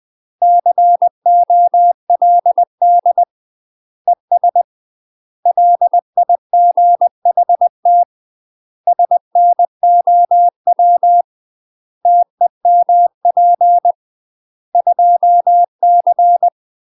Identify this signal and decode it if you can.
Morse code, heard as a keyed tone, about 20 wpm, 700 Hz. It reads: COLD ES LIGHT SNOW TEMP 2C
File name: UK MASTER QSOs TRAINING / 20WPM / UK MASTER QSOs 0053